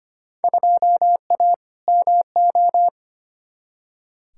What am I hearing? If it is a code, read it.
Morse code, heard as a keyed tone, about 25 wpm, 700 Hz. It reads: 2A MO
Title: Field Day Exchanges Encore - 25wpm